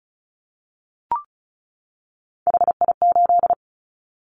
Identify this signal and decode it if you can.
Morse code, heard as a keyed tone, about 34 wpm, 700 Hz. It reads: HI8
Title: Callsign Prefixes for Top 100 DXCC - 35wpm